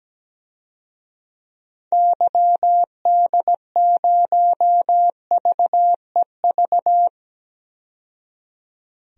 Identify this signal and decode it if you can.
Morse code, heard as a keyed tone, about 17 wpm, 700 Hz. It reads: YD0VEV
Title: DXCC-100 Call Signs - 6 Characters - 17wpm